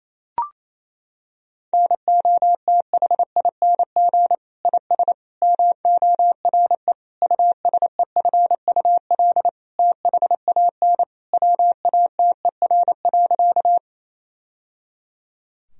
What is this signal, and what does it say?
Morse code, heard as a keyed tone, about 27 wpm, 700 Hz. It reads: NOTHING IS MORE USEFUL THAN WATER.
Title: Sentences from Top 1000 Words - Review Part 4 - 28wpm